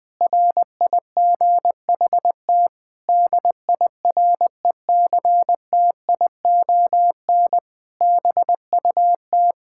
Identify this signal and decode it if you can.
Morse code, heard as a keyed tone, about 20 wpm, 700 Hz. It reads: RIGHT DIRECTION BUT
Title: Sets of 8 Words - 20wpm